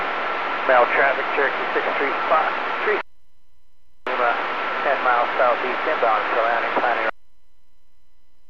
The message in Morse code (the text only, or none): none